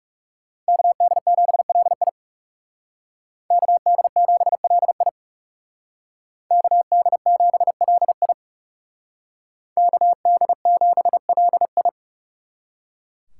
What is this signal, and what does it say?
Morse code, heard as a keyed tone, about 36 wpm, 700 Hz. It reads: KD7LI KD7LI KD7LI KD7LI
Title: Speed Racing - USA 2x2 Call Signs 30wpm